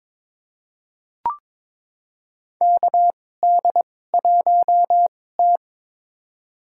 Morse code, heard as a keyed tone, about 23 wpm, spaced as for 16 wpm, 700 Hz. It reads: KD1T
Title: USA Call Signs - 4 Characters - 2x Character Spacing - 22wpm